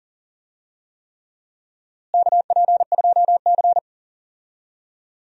Morse code, heard as a keyed tone, about 37 wpm, 700 Hz. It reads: KP2C
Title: DXCC-100 Call Signs - 4 Characters - Repeated 3x - 40wpm